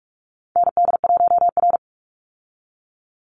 Morse code, heard as a keyed tone, about 44 wpm, 700 Hz. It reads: ND1R
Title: Call Signs Repeated 2x - 45wpm